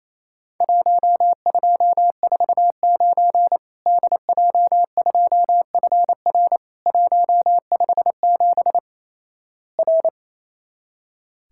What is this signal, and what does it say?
Morse code, heard as a keyed tone, about 28 wpm, 700 Hz. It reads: 1249 DJ2FR 157 R
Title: Worked All Europe DX Contest - QTC Exchange - 28wpm